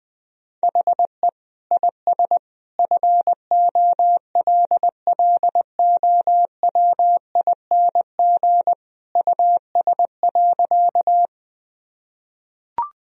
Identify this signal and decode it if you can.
Morse code, heard as a keyed tone, about 20 wpm, 700 Hz. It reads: HE IS FOLLOWING US.
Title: Sentences from Top 1000 Words - Review Part 4 - 20wpm